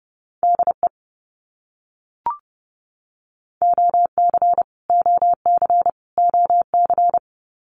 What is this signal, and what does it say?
Morse code, heard as a keyed tone, about 30 wpm, 700 Hz. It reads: DE OC OC OC
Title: Abbreviations Repeated 3x - 30wpm